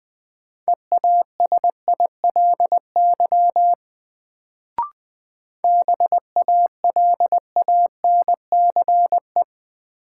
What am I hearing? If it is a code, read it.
Morse code, heard as a keyed tone, about 20 wpm, 700 Hz. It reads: EASILY BALANCE
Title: Sentences from Top 900 Words - New Word Drill - 20wpm